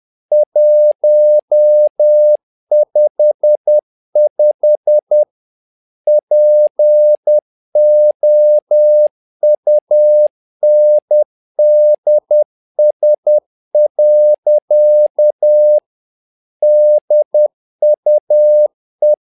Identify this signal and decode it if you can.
Morse code, heard as a keyed tone, about 10 wpm, 600 Hz. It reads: 155 POUNDS. DUE
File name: learning center / tower trouble / tower trouble 10wpm